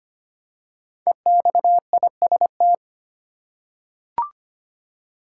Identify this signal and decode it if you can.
Morse code, heard as a keyed tone, about 25 wpm, 700 Hz. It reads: EXIST
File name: 5 Letter Words - First 3 Letters Given - 25wpm